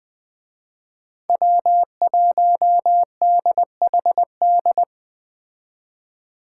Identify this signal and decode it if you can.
Morse code, heard as a keyed tone, about 20 wpm, 700 Hz. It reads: W1DHD